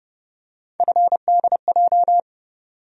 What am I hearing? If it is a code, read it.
Morse code, heard as a keyed tone, about 30 wpm, 700 Hz. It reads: FDJ
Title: USA Call Sign Postfixes - 3 Letters 30wpm